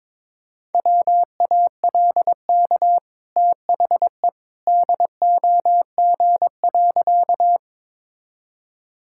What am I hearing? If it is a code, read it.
Morse code, heard as a keyed tone, about 22 wpm, 700 Hz. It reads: WALK THE DOG.
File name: practice / 22wpm / Easy Sentences with Hints - 22wpm